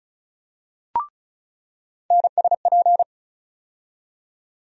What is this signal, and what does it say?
Morse code, heard as a keyed tone, about 36 wpm, 700 Hz. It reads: NSP
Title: USA Call Sign Postfixes - 3 Letters 35wpm